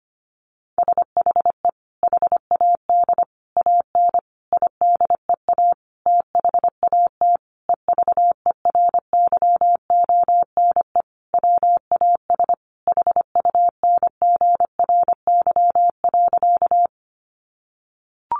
Morse code, heard as a keyed tone, about 25 wpm, 700 Hz. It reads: SHE HAD AN IDEA THAT EVERYONE WAS HUNGRY.
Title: Sentences from Top 1400 Words 25wpm